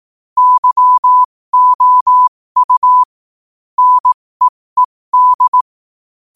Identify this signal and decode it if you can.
Morse code, heard as a keyed tone, about 18 wpm, spaced as for 15 wpm, 1000 Hz. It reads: YOU NEED
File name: qotdfiles / 15wpm